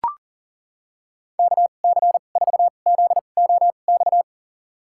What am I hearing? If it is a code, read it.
Morse code, heard as a keyed tone, about 40 wpm, spaced as for 29 wpm, 700 Hz. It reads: KC4ZOX
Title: USA Call Signs - 6 Characters - 2x Character Spacing - 40wpm